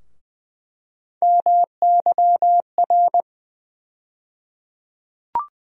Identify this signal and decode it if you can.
Morse code, heard as a keyed tone, about 20 wpm, 700 Hz. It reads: MYR
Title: USA Call Sign Postfixes - 3 Letters 20wpm